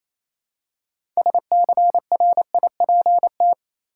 Morse code, heard as a keyed tone, about 28 wpm, 700 Hz. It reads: SCRIPT
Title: Six Letter Words - 28wpm